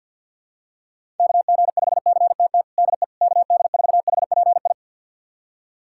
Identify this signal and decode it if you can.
Morse code, heard as a keyed tone, about 51 wpm, 700 Hz. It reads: KG5CTT DE KD4HPI
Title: USA Callsign DE Callsign 50wpm